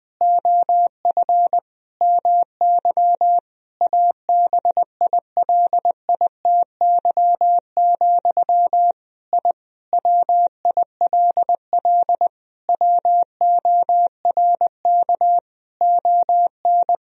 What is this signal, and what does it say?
Morse code, heard as a keyed tone, about 20 wpm, 700 Hz. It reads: OF MY ABILITY, I WILL WORK ON
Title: Sentences from Top 1000 Words - Review Part 3 - 20wpm